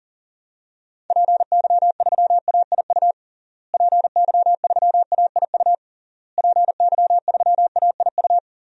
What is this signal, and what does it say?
Morse code, heard as a keyed tone, about 40 wpm, 700 Hz. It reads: PY3AIU PY3AIU PY3AIU
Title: DXCC-100 Call Signs - 6 Characters - Repeated 3x - 40wpm